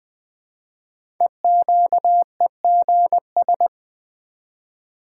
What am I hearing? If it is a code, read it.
Morse code, heard as a keyed tone, about 20 wpm, 700 Hz. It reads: EQEGS
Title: Code Groups - 5 Letters - 20wpm